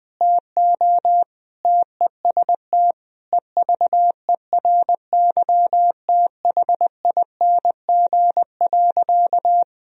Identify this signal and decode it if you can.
Morse code, heard as a keyed tone, about 20 wpm, 700 Hz. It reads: TO TEST EVERYTHING.
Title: Sentences from Top 1000 Words - Review Part 3 - 20wpm